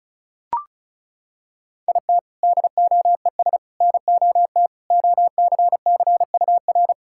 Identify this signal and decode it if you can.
Morse code, heard as a keyed tone, about 34 wpm, 700 Hz. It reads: IT DOES NOT OCCUR
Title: Sentences from Top 1600 Words 35wpm